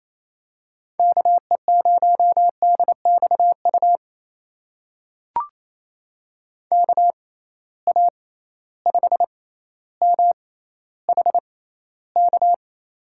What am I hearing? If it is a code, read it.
Morse code, heard as a keyed tone, about 28 wpm, spaced as for 14 wpm, 700 Hz. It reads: KE0DXU KA5MHK
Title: USA Call Signs - 6 Characters - 6x Character Spacing - 28wpm